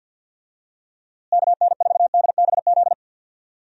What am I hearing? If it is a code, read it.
Morse code, heard as a keyed tone, about 53 wpm, 700 Hz. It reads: KN4DBZ
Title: USA Call Signs - 6 Characters - Repeated 2x - 50wpm